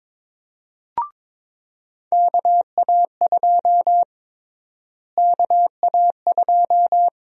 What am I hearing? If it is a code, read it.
Morse code, heard as a keyed tone, about 22 wpm, 700 Hz. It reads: KA2 KA2
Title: USA Call Sign Prefixes - Repeated 2x - 22wpm